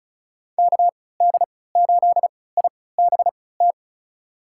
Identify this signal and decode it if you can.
Morse code, heard as a keyed tone, about 36 wpm, spaced as for 20 wpm, 700 Hz. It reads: KD8IBT
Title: USA Call Signs - 6 Characters - 3x Character Spacing - 35wpm